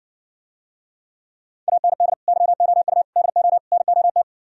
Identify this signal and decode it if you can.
Morse code, heard as a keyed tone, about 60 wpm, 700 Hz. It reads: AND YOU DO NOT